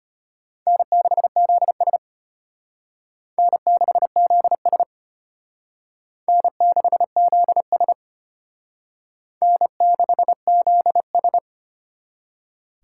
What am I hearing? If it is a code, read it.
Morse code, heard as a keyed tone, about 34 wpm, 700 Hz. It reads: N6ZS N6ZS N6ZS N6ZS